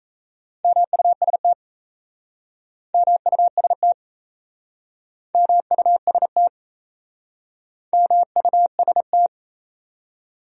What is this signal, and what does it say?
Morse code, heard as a keyed tone, about 37 wpm, 700 Hz. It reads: MUST MUST MUST MUST